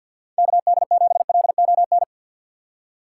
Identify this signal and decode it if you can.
Morse code, heard as a keyed tone, about 53 wpm, 700 Hz. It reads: KD7LON
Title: USA Call Signs - 4x Character Spacing - 50wpm